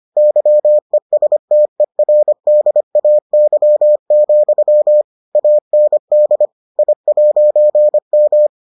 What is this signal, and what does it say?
Morse code, heard as a keyed tone, about 25 wpm, 600 Hz. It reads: YESTERDAY, AND I'M